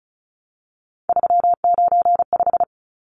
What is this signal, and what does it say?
Morse code, heard as a keyed tone, about 34 wpm, 700 Hz. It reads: 395
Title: Numbers - 3-Digit 35wpm